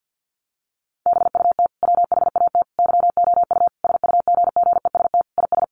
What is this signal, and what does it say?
Morse code, heard as a keyed tone, about 50 wpm, 700 Hz. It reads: BUT WHAT YOU SUGGEST IS
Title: Sentences from Top 1300 Words 50wpm